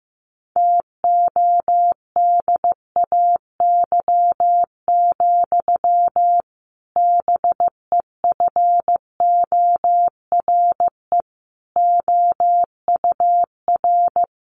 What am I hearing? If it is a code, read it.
Morse code, heard as a keyed tone, about 15 wpm, 700 Hz. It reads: TODAY, BEFORE OUR
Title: A Sound Of Thunder 15wpm